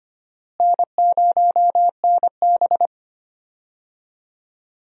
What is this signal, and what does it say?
Morse code, heard as a keyed tone, about 25 wpm, 700 Hz. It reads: N0NB
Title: USA Call Signs - 4 Characters - Repeated 3x - 25wpm